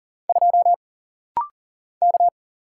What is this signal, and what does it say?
Morse code, heard as a keyed tone, about 40 wpm, 700 Hz. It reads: 2 K
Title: Single Letter-Number - Rapid-Fire 40wpm